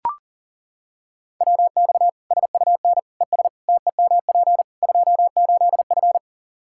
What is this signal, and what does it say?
Morse code, heard as a keyed tone, about 37 wpm, 700 Hz. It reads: WX SUN ES TEMP 28F
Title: Practice QSO - Exchange 2 - Info - 40wpm